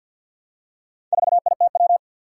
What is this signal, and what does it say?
Morse code, heard as a keyed tone, about 53 wpm, 700 Hz. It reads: 4ITW
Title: Four Challenging Letter-Number Combinations 50wpm